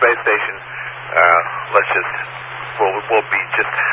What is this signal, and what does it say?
Emergency Alert System (SAME) tones, heard as harsh buzzing bursts, with speech. It is heard from 0.6 to 3.7 s.